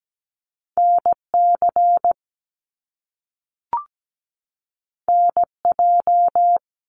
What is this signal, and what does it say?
Morse code, heard as a keyed tone, about 17 wpm, 700 Hz. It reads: NC NJ
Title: US States 17wpm